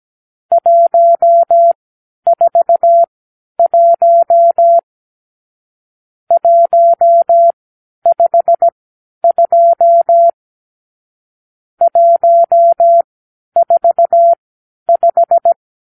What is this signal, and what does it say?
Morse code, heard as a keyed tone, about 17 wpm, spaced as for 10 wpm, 680 Hz. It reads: 141 152 145
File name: arrl / Review (1-5)